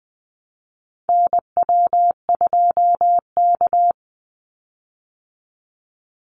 Morse code, heard as a keyed tone, about 20 wpm, 700 Hz. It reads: NW2K
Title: Call Signs Repeated 2x - 20wpm